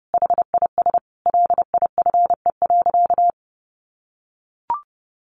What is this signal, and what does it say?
Morse code, heard as a keyed tone, about 30 wpm, 700 Hz. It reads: HIS LIFE.
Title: Sentences from Top 1500 Words 30wpm